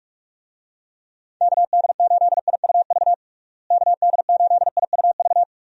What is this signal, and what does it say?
Morse code, heard as a keyed tone, about 46 wpm, 700 Hz. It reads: KD8IUV KD8IUV
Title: USA Call Signs - 6 Characters - Repeated 2x - 45wpm